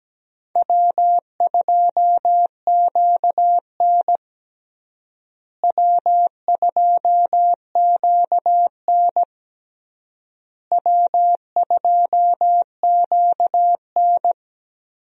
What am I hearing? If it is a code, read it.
Morse code, heard as a keyed tone, about 17 wpm, 700 Hz. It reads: W2QN W2QN W2QN
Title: USA Call Signs - 4 Characters - Repeated 3x - 17wpm